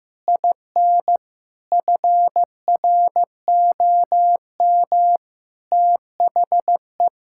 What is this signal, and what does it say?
Morse code, heard as a keyed tone, about 16 wpm, 700 Hz. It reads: IN FROM THE
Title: Sentences from Top 1000 Words - Review Part 5 - 15wpm